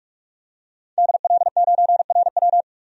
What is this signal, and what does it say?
Morse code, heard as a keyed tone, about 46 wpm, 700 Hz. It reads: DL9RW
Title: DXCC-100 Call Signs - 5 Characters - 45wpm